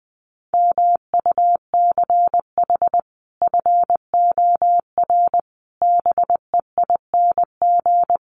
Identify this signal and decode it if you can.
Morse code, heard as a keyed tone, about 20 wpm, 700 Hz. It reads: MUCH FOR BEING
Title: Sets of 5 Words 20wpm